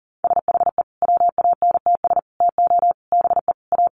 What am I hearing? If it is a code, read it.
Morse code, heard as a keyed tone, about 40 wpm, 700 Hz. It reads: SHE WANTS TO BE A
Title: Easy Phrases 40wpm